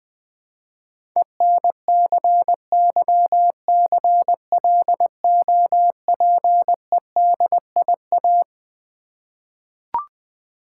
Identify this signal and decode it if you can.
Morse code, heard as a keyed tone, about 21 wpm, 700 Hz. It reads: ENCYCLOPEDIA
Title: Difficult Words with Hints - 20wpm